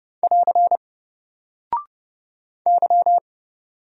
Morse code, heard as a keyed tone, about 30 wpm, 700 Hz. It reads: <AR> Y
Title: Single Character (Letter:Number:Punctuation:Prosign) 30wpm